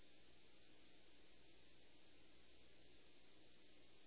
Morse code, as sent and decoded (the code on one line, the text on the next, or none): none